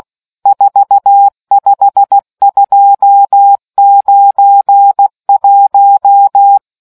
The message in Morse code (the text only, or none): 45291